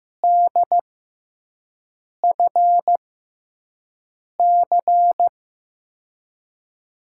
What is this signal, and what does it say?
Morse code, heard as a keyed tone, about 15 wpm, spaced as for 5 wpm, 700 Hz. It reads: DFC
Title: USA Call Signs - 6 Characters - 6x Character Spacing - 15wpm